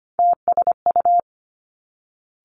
Morse code, heard as a keyed tone, about 25 wpm, 700 Hz. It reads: TSU
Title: Three Challenging Letter-Number Combinations 25wpm